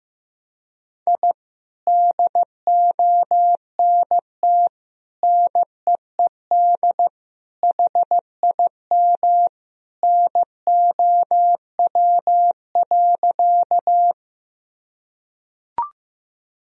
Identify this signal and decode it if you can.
Morse code, heard as a keyed tone, about 15 wpm, 700 Hz. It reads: I DONT NEED HIM NOW.
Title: Sentences from Top 1000 Words - Review Part 1 - 15wpm